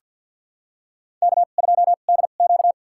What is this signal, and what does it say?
Morse code, heard as a keyed tone, about 51 wpm, spaced as for 36 wpm, 700 Hz. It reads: K2DQ
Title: USA Call Signs - 4 Characters - 2x Character Spacing - 50wpm